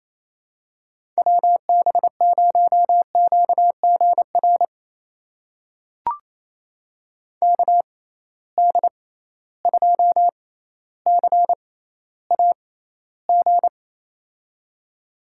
Morse code, heard as a keyed tone, about 28 wpm, spaced as for 14 wpm, 700 Hz. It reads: WB0QGR KD2CAG
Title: USA Call Signs - 6x Character Spacing - 28wpm